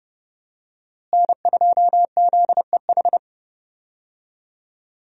Morse code, heard as a keyed tone, about 29 wpm, 700 Hz. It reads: N2ZEH
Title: USA Call Signs - 5 Characters - Repeated 2x - 30wpm